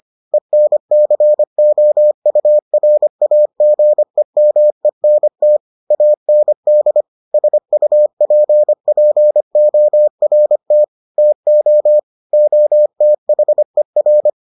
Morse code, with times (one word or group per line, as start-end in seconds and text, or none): 0.3-5.6 ENCOURAGEMENT
5.9-7.0 AND
7.3-10.8 SUPPORT
11.2-12.0 TO
12.3-14.3 OTHER